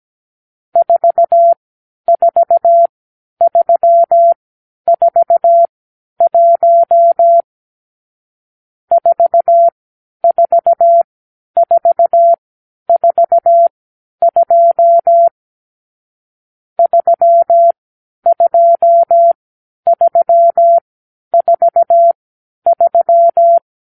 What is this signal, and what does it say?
Morse code, heard as a keyed tone, about 17 wpm, spaced as for 10 wpm, 680 Hz. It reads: 44341 44442 32343
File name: k7qo / Lesson 46 (1-4)